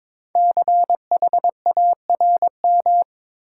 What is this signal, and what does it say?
Morse code, heard as a keyed tone, about 22 wpm, 700 Hz. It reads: CHARM
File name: 5 Letter Words - First 3 Letters Given - 22wpm